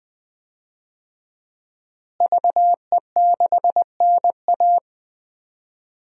Morse code, heard as a keyed tone, about 21 wpm, 700 Hz. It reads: VE6NA
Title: DXCC-100 Call Signs - 5 Characters - Repeated 2x - 20wpm